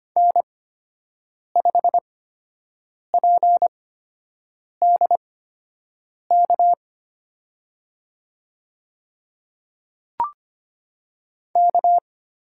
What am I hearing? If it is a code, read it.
Morse code, heard as a keyed tone, about 25 wpm, spaced as for 7 wpm, 700 Hz. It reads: N5PDK K
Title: USA Call Signs - 5 Characters - 8x Character Spacing - 25wpm